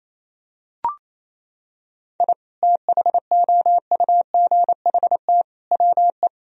Morse code, heard as a keyed tone, about 28 wpm, 700 Hz. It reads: I THOUGHT WE
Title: Sentences from Top 1000 Words - Review Part 2 - 28wpm